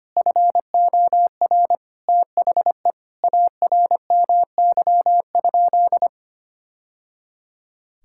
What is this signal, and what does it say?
Morse code, heard as a keyed tone, about 25 wpm, 700 Hz. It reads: FOR THE ARMY?
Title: Sentences from Top 1000 Words - Review Part 5 - 25wpm